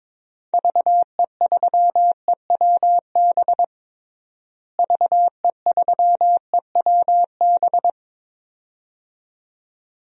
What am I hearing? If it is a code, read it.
Morse code, heard as a keyed tone, about 22 wpm, 700 Hz. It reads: VE3EWB VE3EWB
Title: DXCC-100 Call Signs - 6 Characters - Repeated 2x - 22wpm